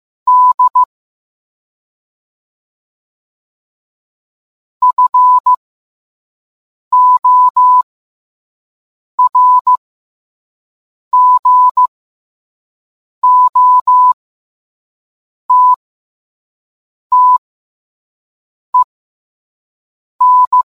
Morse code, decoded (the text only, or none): D FORGOTTEN